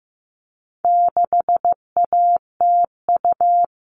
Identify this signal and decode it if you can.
Morse code, heard as a keyed tone, about 15 wpm, 700 Hz. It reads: 6ATU
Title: All Cut Numbers - 4-Digit 15wpm